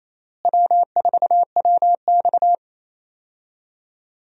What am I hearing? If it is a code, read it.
Morse code, heard as a keyed tone, about 28 wpm, 700 Hz. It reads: W4WX
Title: USA Call Signs - 4 Characters - Repeated 3x - 28wpm